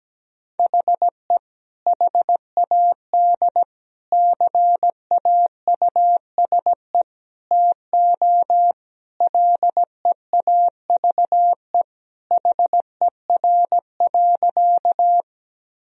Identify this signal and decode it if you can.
Morse code, heard as a keyed tone, about 17 wpm, 700 Hz. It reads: HE HAD CAUSE TO LEAVE HER.
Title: Sentences from Top 1000 Words - Review Part 1 - 17wpm